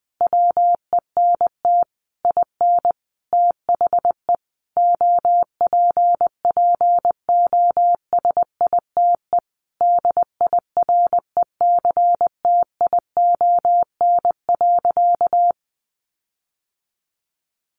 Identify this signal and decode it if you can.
Morse code, heard as a keyed tone, about 20 wpm, 700 Hz. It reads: WENT IN THE OPPOSITE DIRECTION.
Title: Sentences from Top 1300 Words 20wpm